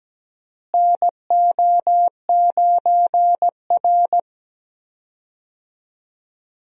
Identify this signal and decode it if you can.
Morse code, heard as a keyed tone, about 17 wpm, 700 Hz. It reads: NO9R
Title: USA Call Signs - 4 Characters - Repeated 2x - 17wpm